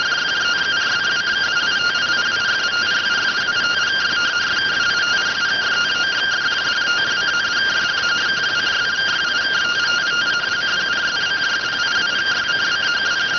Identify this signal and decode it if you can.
Two-tone teleprinter signal, about 170 Hz shift.